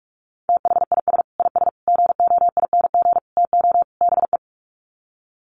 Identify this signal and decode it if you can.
Morse code, heard as a keyed tone, about 44 wpm, 700 Hz. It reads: THIS IS GOING TO BE
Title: Sets of 5 Words 45wpm